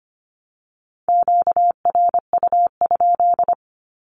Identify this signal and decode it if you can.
Morse code, heard as a keyed tone, about 25 wpm, 700 Hz. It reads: QRU?
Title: Q-Codes 25wpm